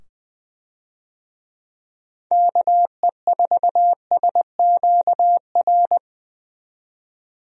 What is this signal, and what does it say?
Morse code, heard as a keyed tone, about 21 wpm, 700 Hz. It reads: KE4SQR
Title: USA Call Signs - 6 Characters - 20wpm